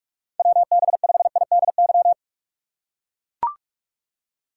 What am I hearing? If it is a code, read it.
Morse code, heard as a keyed tone, about 41 wpm, 700 Hz. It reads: WB5IDY
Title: K1USN SST QSO Practice - 45wpm